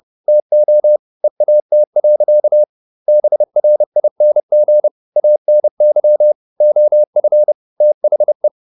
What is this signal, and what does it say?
Morse code, heard as a keyed tone, about 30 wpm, 600 Hz. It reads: TO EAT. BRING ANY OF THE